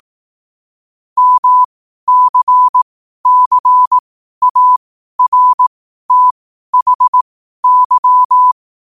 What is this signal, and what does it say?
Morse code, heard as a keyed tone, about 18 wpm, spaced as for 12 wpm, 1000 Hz. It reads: MCCARTHY
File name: qotdfiles / 12wpm